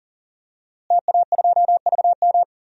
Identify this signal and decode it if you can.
Morse code, heard as a keyed tone, about 38 wpm, 700 Hz. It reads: TA2VM
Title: DXCC-100 Call Signs - 5 Characters - Repeated 3x - 40wpm